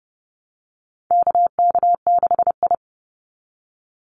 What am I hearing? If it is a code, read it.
Morse code, heard as a keyed tone, about 30 wpm, 700 Hz. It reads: KK6I
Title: Call Signs Repeated 2x - 30wpm